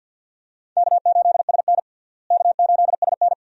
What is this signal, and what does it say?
Morse code, heard as a keyed tone, about 52 wpm, 700 Hz. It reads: K8SN K8SN
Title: USA Call Signs - 4 Characters - Repeated 2x - 50wpm